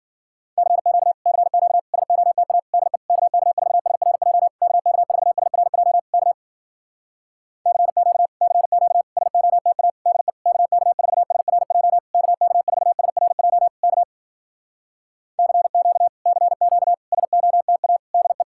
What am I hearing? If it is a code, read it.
Morse code, heard as a keyed tone, about 60 wpm, 700 Hz. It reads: CQ CQ SOTA DE KK4SRJ KK4SRJ K CQ CQ SOTA DE KK4SRJ KK4SRJ K CQ CQ SOTA DE